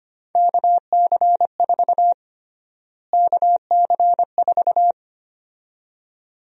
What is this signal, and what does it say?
Morse code, heard as a keyed tone, about 25 wpm, 700 Hz. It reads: KC4 KC4
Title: USA Call Sign Prefixes - Repeated 2x - 25wpm